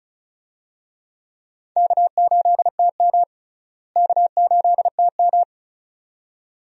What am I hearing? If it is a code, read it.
Morse code, heard as a keyed tone, about 36 wpm, 700 Hz. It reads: K8TM K8TM
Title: USA Call Signs - 4 Characters - Repeated 2x - 35wpm